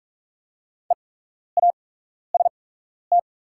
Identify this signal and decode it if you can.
Morse code, heard as a keyed tone, about 44 wpm, spaced as for 12 wpm, 700 Hz. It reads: EAST
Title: 8x Character Spacing - Top 500 Words - 45wpm